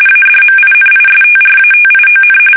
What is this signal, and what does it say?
RTTY, about 45.45 baud. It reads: ADMINISTRATION
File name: fsk850 / usnavy3